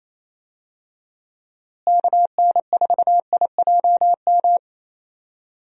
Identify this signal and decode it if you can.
Morse code, heard as a keyed tone, about 28 wpm, 700 Hz. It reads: KN4IJM
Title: USA Call Signs - 6 Characters - Repeated 2x - 28wpm